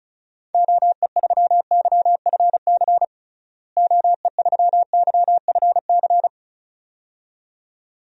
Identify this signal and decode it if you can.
Morse code, heard as a keyed tone, about 34 wpm, 700 Hz. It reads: OE3YFC OE3YFC
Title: DXCC-100 Call Signs - 6 Characters - Repeated 2x - 35wpm